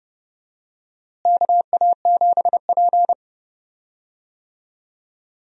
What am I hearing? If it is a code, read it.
Morse code, heard as a keyed tone, about 32 wpm, 700 Hz. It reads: KA7P
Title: USA Call Signs - 4 Characters - Repeated 3x - 30wpm